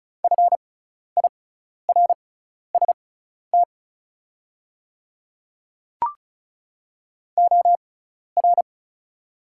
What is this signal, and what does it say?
Morse code, heard as a keyed tone, about 34 wpm, spaced as for 12 wpm, 700 Hz. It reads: FIRST OR
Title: 6x Character Spacing - Top 100 Words - 35wpm